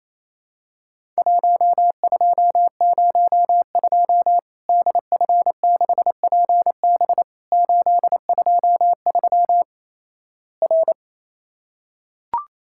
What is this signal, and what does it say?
Morse code, heard as a keyed tone, about 28 wpm, 700 Hz. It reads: 1202 DF6PB 823 R
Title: Worked All Europe DX Contest - QTC Exchange - 28wpm